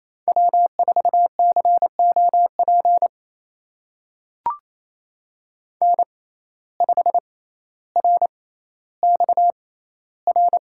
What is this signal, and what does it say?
Morse code, heard as a keyed tone, about 28 wpm, spaced as for 14 wpm, 700 Hz. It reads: W4COP N5RXR
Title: USA Call Signs - 5 Characters - 6x Character Spacing - 28wpm